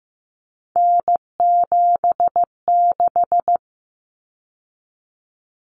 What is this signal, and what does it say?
Morse code, heard as a keyed tone, about 15 wpm, 700 Hz. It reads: N76
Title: Common Cut Numbers - 3-Digit 15wpm